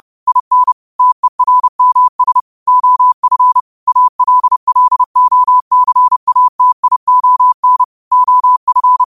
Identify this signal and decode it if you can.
Morse code, heard as a keyed tone, about 30 wpm, 1000 Hz. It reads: IN TERMS OF ALLOCATION OF
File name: qotdfiles / 30wpm